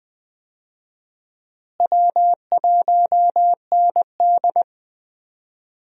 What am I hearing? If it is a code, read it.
Morse code, heard as a keyed tone, about 20 wpm, 700 Hz. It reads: W1ND
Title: USA Call Signs - 4 Characters - Repeated 2x - 20wpm